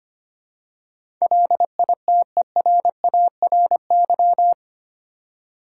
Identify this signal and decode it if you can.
Morse code, heard as a keyed tone, about 25 wpm, 700 Hz. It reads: LITERARY
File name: 8 Letter Words - First 3 Letters Given - 25wpm